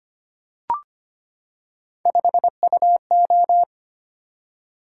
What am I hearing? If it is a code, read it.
Morse code, heard as a keyed tone, about 25 wpm, 700 Hz. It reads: 5UO
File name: Three Letter-Number Combinations - 25wpm